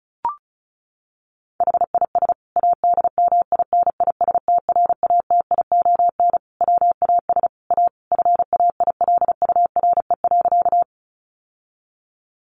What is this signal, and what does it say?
Morse code, heard as a keyed tone, about 34 wpm, 700 Hz. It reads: HIS ADMINISTRATION WAS A FAILURE.
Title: Sentences from Top 1300 Words 35wpm